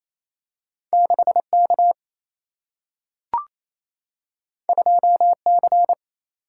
Morse code, heard as a keyed tone, about 28 wpm, 700 Hz. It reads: 6 K 2 C
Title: Single-Letter Single-Number 28wpm